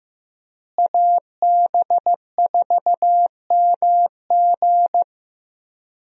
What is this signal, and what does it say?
Morse code, heard as a keyed tone, about 15 wpm, 700 Hz. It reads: AB4MG
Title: USA Call Signs - 5 Characters - Repeated 2x - 15wpm